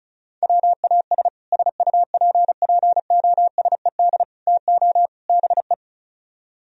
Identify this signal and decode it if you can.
Morse code, heard as a keyed tone, about 42 wpm, 700 Hz. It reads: WAS SUPPOSED TO BE
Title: Speed-Racing - Sets of 4 Words 35wpm